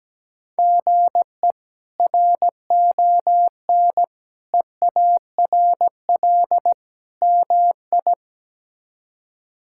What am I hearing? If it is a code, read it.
Morse code, heard as a keyed tone, about 17 wpm, 700 Hz. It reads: GE RON EARL MI
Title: K1USN SST QSO Practice - 17wpm